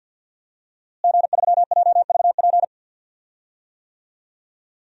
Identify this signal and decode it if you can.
Morse code, heard as a keyed tone, about 47 wpm, 700 Hz. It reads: G3JVP
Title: DXCC-100 Call Signs - 5 Characters - Repeated 2x - 50wpm